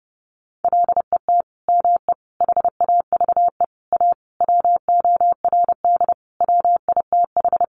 Morse code, heard as a keyed tone, about 30 wpm, 700 Hz. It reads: LET ME HAVE A WORD WITH
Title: Sentences from Top 300 Words 30wpm